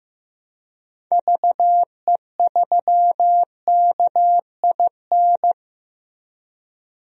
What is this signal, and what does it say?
Morse code, heard as a keyed tone, about 16 wpm, 700 Hz. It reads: VE3KIN
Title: DXCC-100 Call Signs - 6 Characters - Repeated 3x - 15wpm